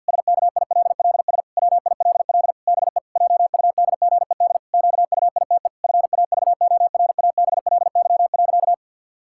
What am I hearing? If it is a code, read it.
Morse code, heard as a keyed tone, about 50 wpm, 700 Hz. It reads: SKILLS WILL BE JUDGED QUITE FAVORABLY.
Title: Pangram Challenge v2 - 50wpm